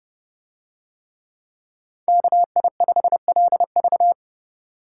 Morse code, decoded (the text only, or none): KI5LV